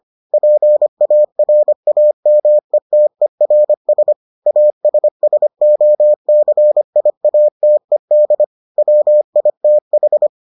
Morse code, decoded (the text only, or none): PARAMETERS ASSOCIATED WITH